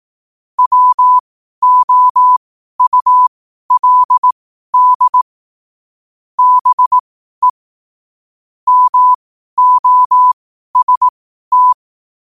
Morse code, heard as a keyed tone, about 18 wpm, spaced as for 12 wpm, 1000 Hz. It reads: WOULD BE MOST
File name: qotdfiles / 12wpm